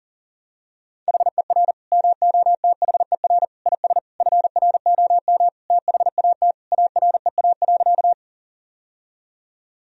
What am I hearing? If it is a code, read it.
Morse code, heard as a keyed tone, about 40 wpm, 700 Hz. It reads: HER MOTHER IS FROM THAT AREA.
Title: Sentences from Top 1000 Words - Review Part 2 - 40wpm